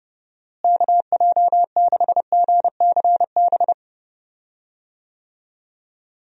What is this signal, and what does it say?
Morse code, heard as a keyed tone, about 32 wpm, 700 Hz. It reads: KJ6GCB
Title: USA Call Signs - 6 Characters - Repeated 2x - 30wpm